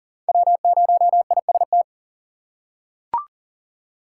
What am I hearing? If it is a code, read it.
Morse code, heard as a keyed tone, about 40 wpm, 700 Hz. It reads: W0IST
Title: USA Call Signs - 5 Characters - 6x Character Spacing - 40wpm